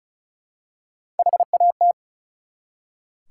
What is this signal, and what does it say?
Morse code, heard as a keyed tone, about 36 wpm, 700 Hz. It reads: HAT
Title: Three Letter Words - 35wpm